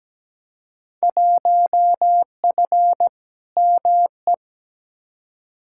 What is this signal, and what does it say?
Morse code, heard as a keyed tone, about 17 wpm, 700 Hz. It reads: 1F ME
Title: Field Day Exchanges Encore - 17wpm